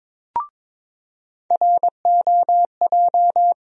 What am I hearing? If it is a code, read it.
Morse code, heard as a keyed tone, about 22 wpm, 700 Hz. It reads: ROJ
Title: USA Call Sign Postfixes - 3 Letters 22wpm